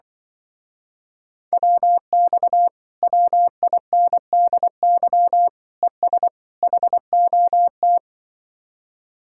Morse code, heard as a keyed tone, about 25 wpm, 700 Hz. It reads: WX WINDY ES HOT
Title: Speed-Racing Common QSO Phrases - 20wpm